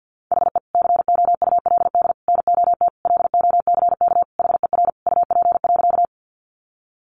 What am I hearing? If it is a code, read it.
Morse code, heard as a keyed tone, about 50 wpm, 700 Hz. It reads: HE COULD NOT LOCK HER UP.
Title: Sentences from Top 1300 Words 50wpm